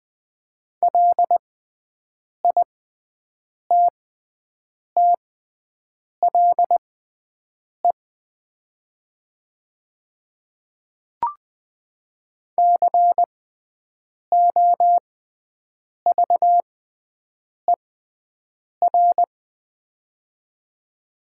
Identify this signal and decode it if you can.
Morse code, heard as a keyed tone, about 20 wpm, spaced as for 7 wpm, 700 Hz. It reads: LITTLE COVER
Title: 6x Character Spacing - Top 300 Words - 20wpm